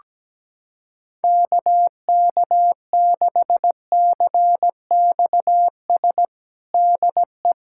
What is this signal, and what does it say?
Morse code, heard as a keyed tone, about 17 wpm, 700 Hz. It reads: KK6CXS DE
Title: USA Callsign DE Callsign - 6 Characters - 17wpm